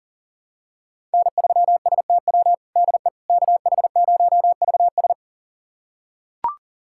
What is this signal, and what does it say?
Morse code, heard as a keyed tone, about 37 wpm, 700 Hz. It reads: N3STW DE KH0VS
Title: USA Callsign DE Callsign - 5 Characters - 40wpm